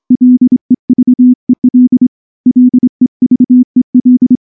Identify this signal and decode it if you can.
Morse code, heard as a keyed tone, about 24 wpm, 250 Hz. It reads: LEVEL LEVEL
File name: morse / morse-training-25WPM